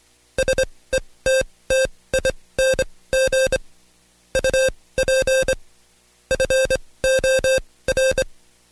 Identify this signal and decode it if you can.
Morse code, heard as a keyed tone, about 24 wpm, spaced as for 18 wpm, 530 Hz. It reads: SETTING UP FOR
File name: Code Quick / AudioDownld2 / Seg135-Stepper8 17WPM